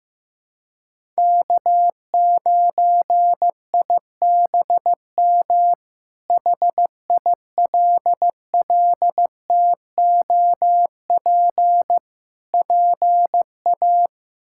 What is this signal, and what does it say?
Morse code, heard as a keyed tone, about 15 wpm, 700 Hz. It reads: K9IBM HILLTOP PA
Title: K1USN SST QSO Practice - 15wpm